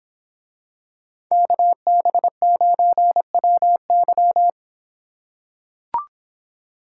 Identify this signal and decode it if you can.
Morse code, heard as a keyed tone, about 26 wpm, 700 Hz. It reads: KB9WY